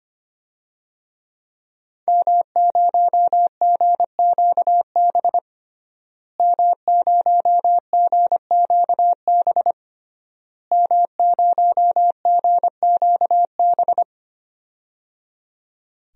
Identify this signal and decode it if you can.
Morse code, heard as a keyed tone, about 25 wpm, 700 Hz. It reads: M0GQB M0GQB M0GQB
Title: DXCC-100 Call Signs - 5 Characters - Repeated 3x - 25wpm